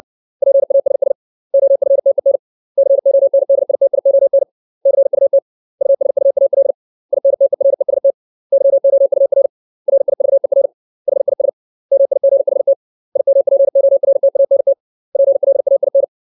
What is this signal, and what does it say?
Morse code, heard as a keyed tone, about 60 wpm, 550 Hz. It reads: PASS OFTEN CONDITION CUT USUAL INTEREST YOUR RIVER HIS MIGHT IMPORTANT PLAIN
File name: 632 words / 632 Words 60WPM